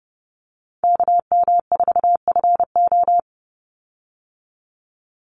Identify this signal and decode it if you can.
Morse code, heard as a keyed tone, about 30 wpm, 700 Hz. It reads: KM4FO
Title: Call Signs Repeated 3x - 30wpm